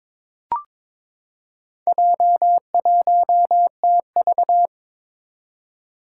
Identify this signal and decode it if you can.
Morse code, heard as a keyed tone, about 22 wpm, 700 Hz. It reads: J1TV
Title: Four Challenging Letter-Number Combinations 22wpm